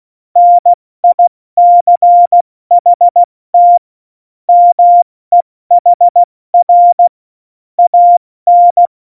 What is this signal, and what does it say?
Morse code, heard as a keyed tone, about 16 wpm, 700 Hz. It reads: NICHT MEHR AN
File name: Morsekurs / Witze / Witz 26 Mai, 2020 70 F80